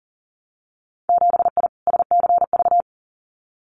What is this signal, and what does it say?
Morse code, heard as a keyed tone, about 38 wpm, 700 Hz. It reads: 7I SCV
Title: Winter Field Day Exchanges Encore 40wpm